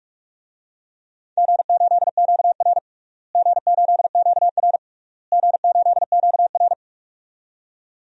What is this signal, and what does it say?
Morse code, heard as a keyed tone, about 46 wpm, 700 Hz. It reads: G8QR G8QR G8QR
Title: DXCC-100 Call Signs - 4 Characters - Repeated 3x - 45wpm